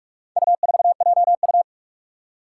Morse code, heard as a keyed tone, about 43 wpm, 700 Hz. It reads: U4JU
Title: Four Challenging Letter-Number Combinations 45wpm